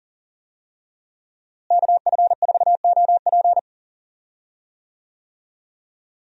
Morse code, heard as a keyed tone, about 40 wpm, 700 Hz. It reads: KF4OP
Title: USA Call Signs - 5 Characters - Repeated 3x - 40wpm